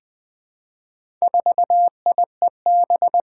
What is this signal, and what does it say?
Morse code, heard as a keyed tone, about 21 wpm, 700 Hz. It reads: 4IEB
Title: Four Challenging Letter-Number Combinations 20wpm